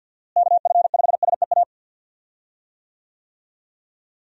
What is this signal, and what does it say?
Morse code, heard as a keyed tone, about 51 wpm, 700 Hz. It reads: KF5SEA
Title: USA Call Signs - 6 Characters - Repeated 3x - 50wpm